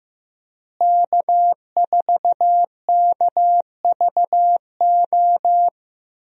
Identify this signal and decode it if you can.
Morse code, heard as a keyed tone, about 15 wpm, 700 Hz. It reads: K4KVO
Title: USA Call Signs - 5 Characters - Repeated 3x - 15wpm